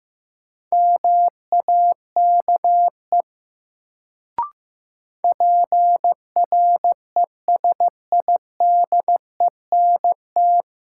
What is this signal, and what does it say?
Morse code, heard as a keyed tone, about 15 wpm, 700 Hz. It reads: MAKE PRESIDENT
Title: Sets of 2 Words - New Word Drill - 15wpm